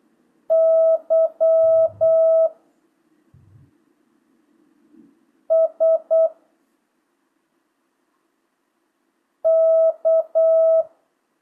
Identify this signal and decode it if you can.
Morse code, heard as a keyed tone, about 8 wpm, 650 Hz. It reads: Y S K
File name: jh8cbh / 2025cw